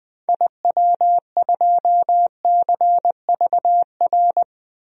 Morse code, heard as a keyed tone, about 21 wpm, 700 Hz. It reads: IW2CVR
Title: DXCC-100 Call Signs - 6 Characters - 20wpm